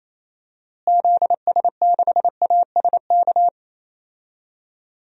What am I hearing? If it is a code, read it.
Morse code, heard as a keyed tone, about 27 wpm, 700 Hz. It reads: ZS6ASK
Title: DXCC-100 Call Signs - 6 Characters - Repeated 3x - 28wpm